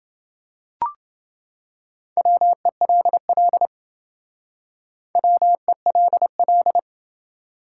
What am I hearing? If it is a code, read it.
Morse code, heard as a keyed tone, about 27 wpm, 700 Hz. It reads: WELL WELL